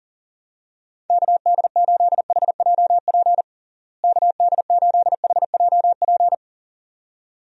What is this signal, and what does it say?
Morse code, heard as a keyed tone, about 37 wpm, 700 Hz. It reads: KD8HJP KD8HJP
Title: USA Call Signs - 6 Characters - Repeated 2x - 40wpm